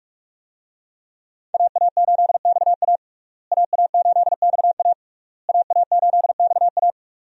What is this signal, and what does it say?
Morse code, heard as a keyed tone, about 46 wpm, 700 Hz. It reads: AA8XA AA8XA AA8XA
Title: USA Call Signs - 5 Characters - Repeated 3x - 45wpm